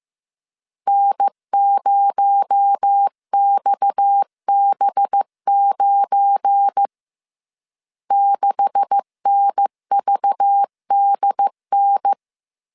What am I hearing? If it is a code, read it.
Morse code, heard as a keyed tone, about 15 wpm, 800 Hz. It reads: N0XB9 6NVDN
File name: CW / DET / 1 9 1500C S C 14wpm 3245